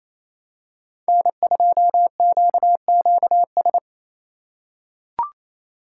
Morse code, heard as a keyed tone, about 28 wpm, 700 Hz. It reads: N2QQS
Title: USA Call Signs - 5 Characters - 2x Character Spacing - 28wpm